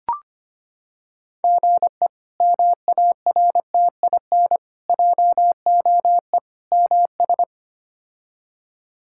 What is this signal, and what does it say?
Morse code, heard as a keyed tone, about 25 wpm, 700 Hz. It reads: GE MARTIN JOE MS
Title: K1USN SST QSO Practice - 25wpm